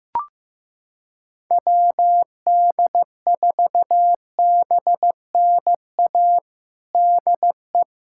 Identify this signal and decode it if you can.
Morse code, heard as a keyed tone, about 15 wpm, 700 Hz. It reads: WD4BNA DE
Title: USA Callsign DE Callsign - 6 Characters - 15wpm